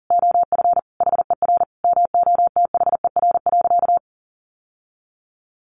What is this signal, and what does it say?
Morse code, heard as a keyed tone, about 40 wpm, 700 Hz. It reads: OF HER MOTHER.
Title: Sentences from Top 1100 Words 40wpm